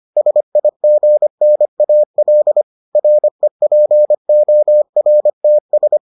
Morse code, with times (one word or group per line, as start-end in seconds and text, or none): 0.2-2.6 SIGNAL
3.0-6.0 REPORTS